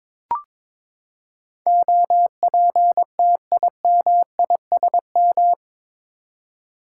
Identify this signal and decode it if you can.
Morse code, heard as a keyed tone, about 23 wpm, 700 Hz. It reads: OPTIMISM
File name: practice / 22wpm / Eight Letter Words - 22wpm